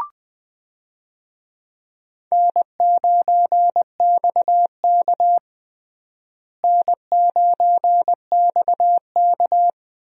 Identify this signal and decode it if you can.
Morse code, heard as a keyed tone, about 21 wpm, 700 Hz. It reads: N9XK N9XK
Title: USA Call Signs - 4 Characters - Repeated 2x - 20wpm